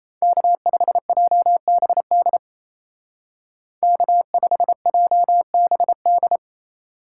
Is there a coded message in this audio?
Yes: Morse code from 0.2 to 6.4 s, heard as a keyed tone, about 36 wpm, 700 Hz. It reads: K5JBD K5JBD